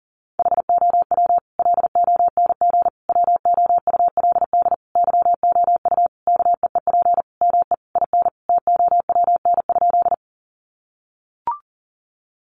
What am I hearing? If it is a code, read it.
Morse code, heard as a keyed tone, about 40 wpm, 700 Hz. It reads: HOW LONG WOULD YOU KEEP ME IN TOWN?
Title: Sentences from Top 500 Words 40wpm